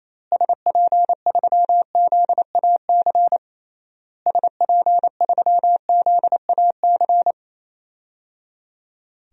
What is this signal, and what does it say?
Morse code, heard as a keyed tone, about 28 wpm, 700 Hz. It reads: SP3ZAC SP3ZAC
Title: DXCC-100 Call Signs - 6 Characters - Repeated 3x - 28wpm